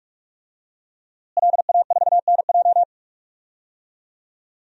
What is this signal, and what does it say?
Morse code, heard as a keyed tone, about 46 wpm, 700 Hz. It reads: LA4NJ
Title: DXCC-100 Call Signs - 5 Characters - Repeated 2x - 45wpm